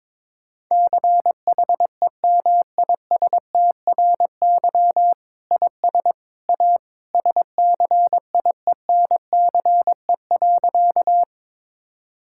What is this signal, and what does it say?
Morse code, heard as a keyed tone, about 22 wpm, 700 Hz. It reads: CHEMISTRY IS A SCIENCE.
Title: Sentences from Top 1000 Words - Review Part 6 - 22wpm